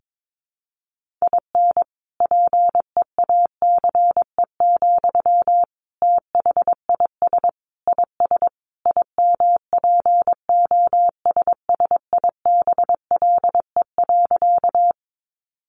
Morse code, heard as a keyed tone, about 22 wpm, 700 Hz. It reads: IN PEACE, THIS IS IMPOSSIBLE.
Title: Sentences from Top 1500 Words 22wpm